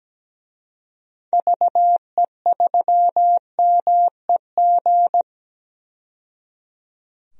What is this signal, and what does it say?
Morse code, heard as a keyed tone, about 17 wpm, 700 Hz. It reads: VE3MEG
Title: DXCC-100 Call Signs - 6 Characters - 17wpm